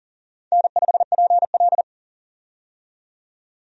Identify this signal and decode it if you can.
Morse code, heard as a keyed tone, about 40 wpm, 700 Hz. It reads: N5PL
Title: USA Call Signs - 4 Characters - 40wpm